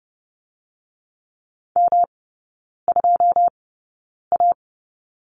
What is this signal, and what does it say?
Morse code, heard as a keyed tone, about 30 wpm, spaced as for 9 wpm, 700 Hz. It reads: M2A
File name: Lesson 18e - Letter m - Three Characters 30wpm